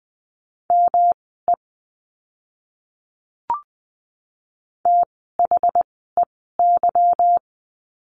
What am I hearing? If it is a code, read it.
Morse code, heard as a keyed tone, about 20 wpm, spaced as for 15 wpm, 700 Hz. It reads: ME THEY
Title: 160 Lesson 26f - Letter g - Words 20wpm